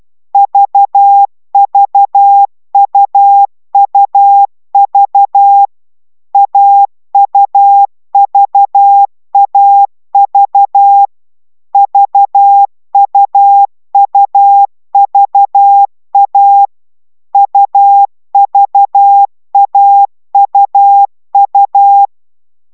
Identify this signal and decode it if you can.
Morse code, heard as a keyed tone, about 12 wpm, 800 Hz. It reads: VVUUV AUVAV VUUVA UVAUU